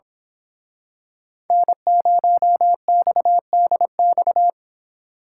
Morse code, heard as a keyed tone, about 26 wpm, 700 Hz. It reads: N0XDX
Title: Speed Racing - USA Call Signs 22wpm